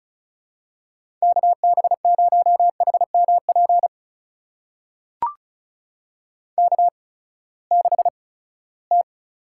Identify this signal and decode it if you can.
Morse code, heard as a keyed tone, about 36 wpm, spaced as for 20 wpm, 700 Hz. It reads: KB0HMP K6T
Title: USA Call Signs - 8x Character Spacing - 35wpm